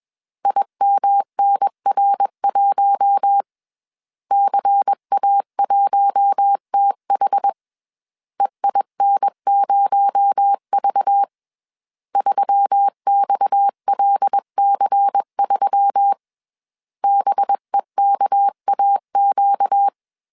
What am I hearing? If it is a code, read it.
Morse code, heard as a keyed tone, about 21 wpm, 800 Hz. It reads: IMNR1 CA1TH EIN0V 3XLC3 BEKAQ
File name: CW / DET / 1 6 1500C S C 20wpm 2116